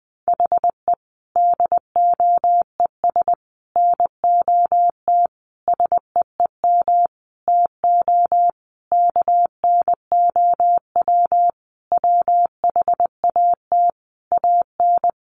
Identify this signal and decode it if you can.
Morse code, heard as a keyed tone, about 20 wpm, 700 Hz. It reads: HE DOES NOT SEEM TO KNOW WHAT AN
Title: Sentences from Top 1400 Words 20wpm